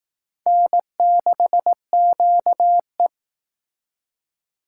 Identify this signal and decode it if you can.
Morse code, heard as a keyed tone, about 18 wpm, 700 Hz. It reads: N6QE
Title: Speed Racing - USA 1x2 Call Signs 15wpm